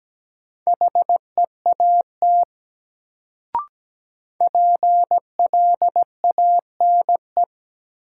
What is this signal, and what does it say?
Morse code, heard as a keyed tone, about 17 wpm, 700 Hz. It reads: HEAT PLANE
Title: Sentences from Top 1000 Words - Review Part 3 - New Word Drill - 17wpm